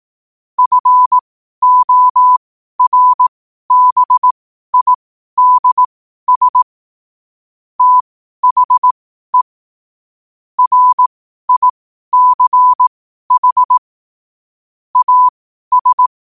Morse code, heard as a keyed tone, about 18 wpm, spaced as for 12 wpm, 1000 Hz. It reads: FORBIDS THE RICH AS
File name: qotdfiles / 12wpm